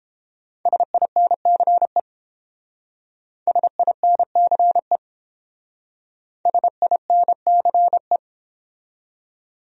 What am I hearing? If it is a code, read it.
Morse code, heard as a keyed tone, about 28 wpm, 700 Hz. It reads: SINCE SINCE SINCE